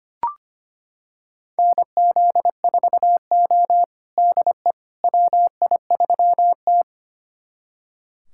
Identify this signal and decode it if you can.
Morse code, heard as a keyed tone, about 25 wpm, 700 Hz. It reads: NZ4O DE WI3T
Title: USA Callsign DE Callsign - 4 Characters - 25wpm